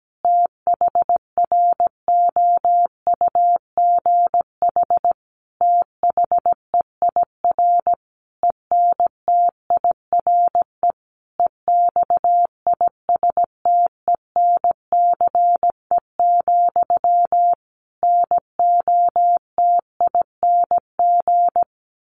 Morse code, heard as a keyed tone, about 17 wpm, 700 Hz. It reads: THROUGH THEIR ENTIRE EXISTENCE, NOTING
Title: A Sound Of Thunder 17wpm